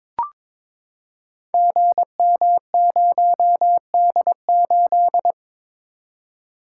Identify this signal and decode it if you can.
Morse code, heard as a keyed tone, about 22 wpm, 700 Hz. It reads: GM0D8
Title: Code Groups - 5 Letter:Numbers - 22wpm